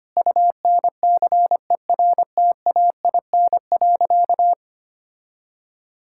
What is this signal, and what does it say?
Morse code, heard as a keyed tone, about 25 wpm, 700 Hz. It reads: UNCERTAIN.
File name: practice / 25wpm / Sentences with 3-Word Phrases - 25wpm